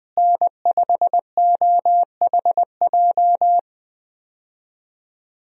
Morse code, heard as a keyed tone, about 21 wpm, 700 Hz. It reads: N5OHJ
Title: USA Call Signs - 5 Characters - Repeated 3x - 20wpm